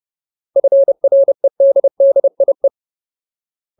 Morse code, heard as a keyed tone, about 30 wpm, 550 Hz. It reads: FREDDIE
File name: Ultimate Word List / 30WPM Ultimate